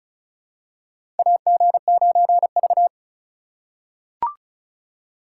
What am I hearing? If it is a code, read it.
Morse code, heard as a keyed tone, about 36 wpm, 700 Hz. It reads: AG9V
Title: USA Call Signs - 4 Characters - 3x Character Spacing - 35wpm